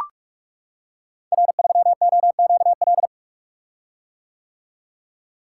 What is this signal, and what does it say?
Morse code, heard as a keyed tone, about 44 wpm, 700 Hz. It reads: R3OQL
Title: Code Groups - 5 Letter:Numbers - 45wpm